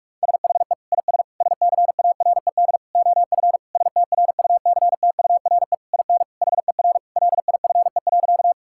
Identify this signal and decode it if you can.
Morse code, heard as a keyed tone, about 46 wpm, 700 Hz. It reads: SHE IS SCARED OF STRUCTURE IN HER LIFE.
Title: Sentences from Top 1000 Words - Review Part 6 - 45wpm